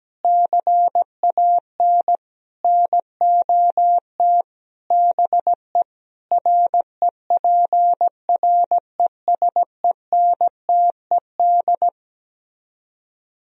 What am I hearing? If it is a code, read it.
Morse code, heard as a keyed tone, about 17 wpm, 700 Hz. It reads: CAN NOT BE REPRESENTED
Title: Speed-Racing - Sets of 5 Words 17wpm